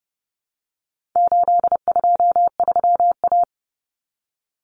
Morse code, heard as a keyed tone, about 28 wpm, 700 Hz. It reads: 823A
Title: Common Cut Numbers - 4-Digit 30wpm